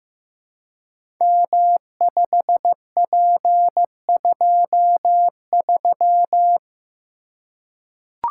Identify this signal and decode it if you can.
Morse code, heard as a keyed tone, about 15 wpm, 700 Hz. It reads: M5P23
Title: Code Groups - 5 Letter:Numbers - 15wpm